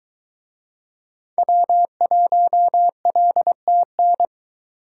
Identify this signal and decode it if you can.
Morse code, heard as a keyed tone, about 23 wpm, 700 Hz. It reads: W1LTN